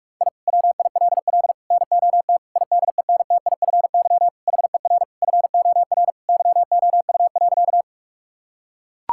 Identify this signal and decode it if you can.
Morse code, heard as a keyed tone, about 41 wpm, 700 Hz. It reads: I WILL NOT IDENTIFY HER FOR YOU.
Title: Sentences from Top 1000 Words - Review Part 6 - 45wpm